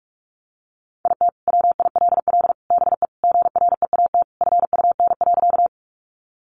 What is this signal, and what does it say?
Morse code, heard as a keyed tone, about 44 wpm, 700 Hz. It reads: IT WILL BE GREAT FUN.
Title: Sentences from Top 700 Words 45wpm